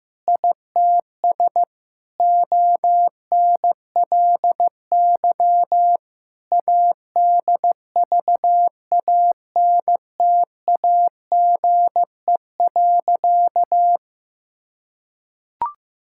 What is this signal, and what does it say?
Morse code, heard as a keyed tone, about 15 wpm, 700 Hz. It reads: ITS ONLY ADVANTAGE.
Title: Sentences from Top 1000 Words - Review Part 6 - 15wpm